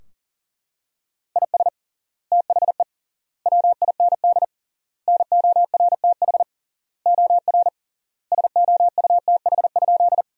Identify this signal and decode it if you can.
Morse code, heard as a keyed tone, about 40 wpm, 700 Hz. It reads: IS THE WIND NORTH OR SOUTH?
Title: 3x Word-Spacing - Sentences from Top 400 Words - 40wpm